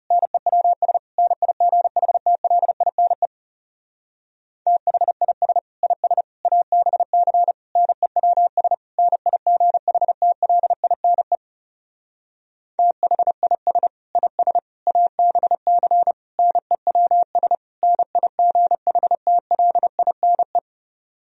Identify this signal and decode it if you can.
Morse code, heard as a keyed tone, about 34 wpm, 700 Hz. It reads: NEWS NIGHTLINE THIS IS ABC NEWS NIGHTLINE THIS IS ABC NEWS NIGHTLINE
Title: Speed-Racing - Sets of 5 Words 30wpm